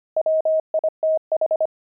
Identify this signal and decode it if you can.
Morse code, heard as a keyed tone, about 25 wpm, 630 Hz. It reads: WITH
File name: pangram-fs-25-cs-25-aardvark